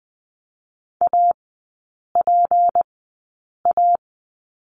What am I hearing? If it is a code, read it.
Morse code, heard as a keyed tone, about 20 wpm, spaced as for 8 wpm, 700 Hz. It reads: APA
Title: 191 Lesson 31e - Letter v - Three Characters 20wpm